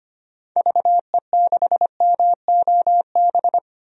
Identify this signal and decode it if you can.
Morse code, heard as a keyed tone, about 25 wpm, 700 Hz. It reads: VE6MOB
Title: DXCC-100 Call Signs - 6 Characters - 25wpm